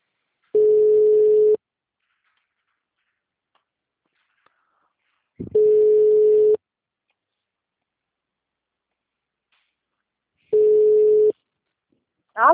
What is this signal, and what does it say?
Ringback tone (Europe, 425 Hz).
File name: out- -2023- - - .94